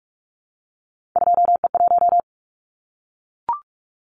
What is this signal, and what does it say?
Morse code, heard as a keyed tone, about 44 wpm, 700 Hz. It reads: 2E1
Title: Callsign Prefixes for Top 25 DXCC - Repeated 3x - 45wpm